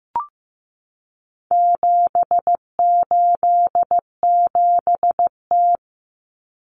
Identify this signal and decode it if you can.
Morse code, heard as a keyed tone, about 15 wpm, 700 Hz. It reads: 787T
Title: Common Cut Numbers - 4-Digit 15wpm